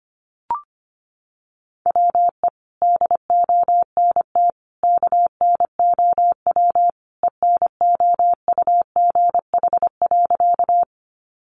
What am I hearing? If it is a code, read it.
Morse code, heard as a keyed tone, about 25 wpm, 700 Hz. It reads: WE DONT KNOW ENOUGH.
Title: Sentences from Top 400 Words 25wpm